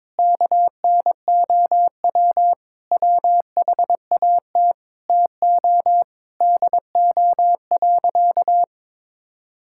Morse code, heard as a keyed tone, about 21 wpm, 700 Hz. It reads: KNOW WHAT TO DO.
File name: practice / 22wpm / Sentences with 4-Word Phrases - 22wpm